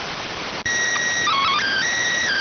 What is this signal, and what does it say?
SSTV (slow-scan television), PD 290.